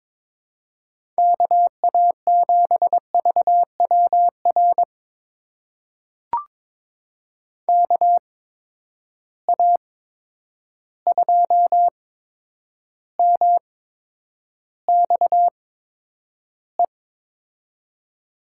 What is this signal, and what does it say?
Morse code, heard as a keyed tone, about 22 wpm, spaced as for 9 wpm, 700 Hz. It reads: KA7VWR KA2MXE
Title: USA Call Signs - 6 Characters - 8x Character Spacing - 22wpm